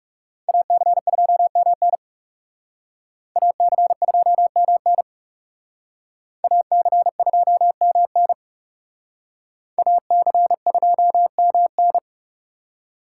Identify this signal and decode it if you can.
Morse code, heard as a keyed tone, about 38 wpm, 700 Hz. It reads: AC2MN AC2MN AC2MN AC2MN